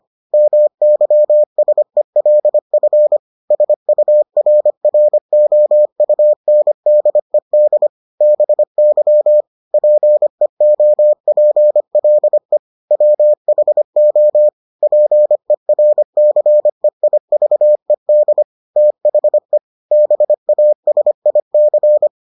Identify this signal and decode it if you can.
Morse code, heard as a keyed tone, about 25 wpm, 600 Hz. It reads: MYSELF SURROUNDED BY PEOPLE WHO PERCEIVED THE BASIC